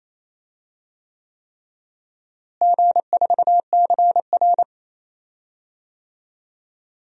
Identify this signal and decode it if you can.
Morse code, heard as a keyed tone, about 28 wpm, 700 Hz. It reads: G4CR
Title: DXCC-100 Call Signs - 4 Characters - Repeated 3x - 28wpm